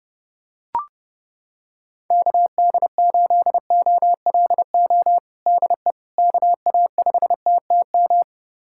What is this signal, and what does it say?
Morse code, heard as a keyed tone, about 30 wpm, 700 Hz. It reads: KD8OLO DE KA5TTM
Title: USA Callsign DE Callsign 30wpm